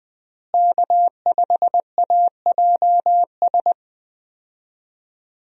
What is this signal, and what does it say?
Morse code, heard as a keyed tone, about 20 wpm, 700 Hz. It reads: K5AJS
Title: USA Call Signs - 5 Characters - Repeated 3x - 20wpm